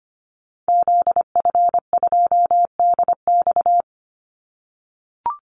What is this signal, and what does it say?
Morse code, heard as a keyed tone, about 25 wpm, 700 Hz. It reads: ZF2DX
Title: Call Signs - 25wpm